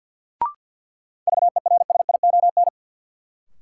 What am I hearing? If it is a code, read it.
Morse code, heard as a keyed tone, about 52 wpm, 700 Hz. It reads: VERSION
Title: Sentences from Top 1400 Words - New Word Drill Alt - 50wpm